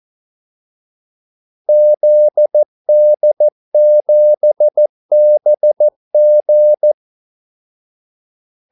Morse code, heard as a keyed tone, about 14 wpm, 600 Hz. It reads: ZD7BG
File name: Callsigns-14wpm